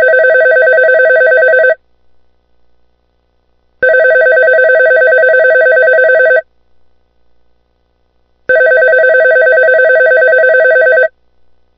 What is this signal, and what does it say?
There is no code.